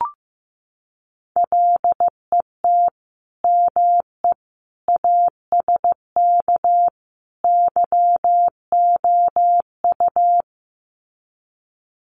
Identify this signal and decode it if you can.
Morse code, heard as a keyed tone, about 15 wpm, 700 Hz. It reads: LET ME ASK YOU
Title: Sets of 4 Words 15wpm